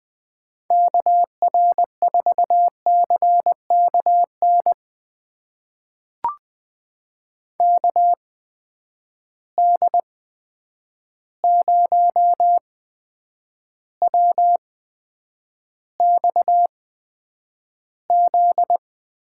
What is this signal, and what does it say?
Morse code, heard as a keyed tone, about 20 wpm, spaced as for 9 wpm, 700 Hz. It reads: KR4CKN KD0WXZ
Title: USA Call Signs - 6 Characters - 8x Character Spacing - 20wpm